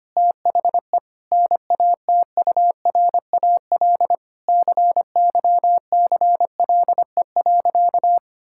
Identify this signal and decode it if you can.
Morse code, heard as a keyed tone, about 25 wpm, 700 Hz. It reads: THE NATURAL CYCLE.
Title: Sentences from Top 1000 Words - Review Part 6 - 25wpm